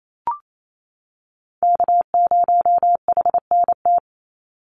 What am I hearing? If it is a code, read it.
Morse code, heard as a keyed tone, about 28 wpm, 700 Hz. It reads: K0HNT
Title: USA Call Signs 28wpm